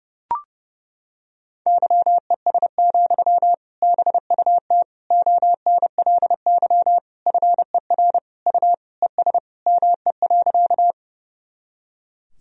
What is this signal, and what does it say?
Morse code, heard as a keyed tone, about 32 wpm, 700 Hz. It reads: YES, BUT ONLY FER U ES ME.
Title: Sentences from Top 500 Words and Abbreviations 30wpm